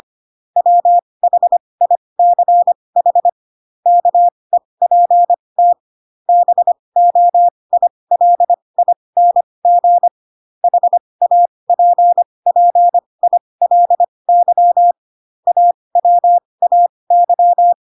Morse code, heard as a keyed tone, about 25 wpm, spaced as for 20 wpm, 700 Hz. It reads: WHICH KEPT BOILING HAPPILY AWAY
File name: Pinocchio / Pinocchio ch3 15 25 ik1hge